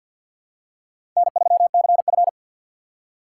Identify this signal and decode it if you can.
Morse code, heard as a keyed tone, about 54 wpm, 700 Hz. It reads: N3CF
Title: USA Call Signs - 4 Characters - 6x Character Spacing - 50wpm